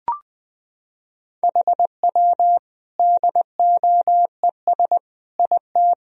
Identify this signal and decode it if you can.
Morse code, heard as a keyed tone, about 20 wpm, 700 Hz. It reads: HW DOES IT
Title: Sentences from Top 400 Words and Abbreviations 20wpm